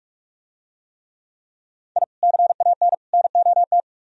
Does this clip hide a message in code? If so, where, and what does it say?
Yes: Morse code from 2.0 to 3.8 s, heard as a keyed tone, about 44 wpm, 700 Hz. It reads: I CAN NOT